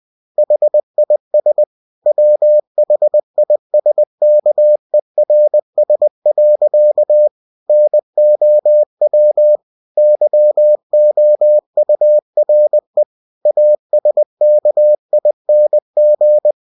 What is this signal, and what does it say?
Morse code, heard as a keyed tone, about 20 wpm, 600 Hz. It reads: HIS WHISKERS. NOW YOURE ASKING
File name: cw-books / winnie-the-pooh / winnie-20wpm-page-0084